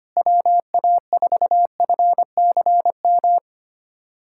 Morse code, heard as a keyed tone, about 25 wpm, 690 Hz. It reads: WA4FCM
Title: SOTA QSO Practice 25wpm